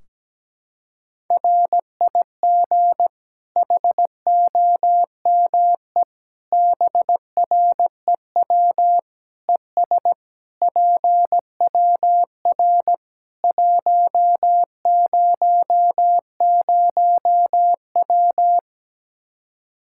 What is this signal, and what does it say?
Morse code, heard as a keyed tone, about 17 wpm, 700 Hz. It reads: RIG HOME BREW ES PWR 100W
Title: Practice QSO - Exchange 2 - Info - 17wpm